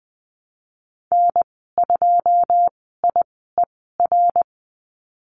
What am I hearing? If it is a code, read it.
Morse code, heard as a keyed tone, about 19 wpm, spaced as for 14 wpm, 700 Hz. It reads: N2IER
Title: 173 Lesson 28g - Number 9 - Call Signs 20wpm